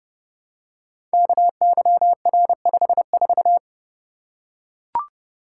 Morse code, heard as a keyed tone, about 28 wpm, 700 Hz. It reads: KYR54
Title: Code Groups - 5 Letter:Numbers - 30wpm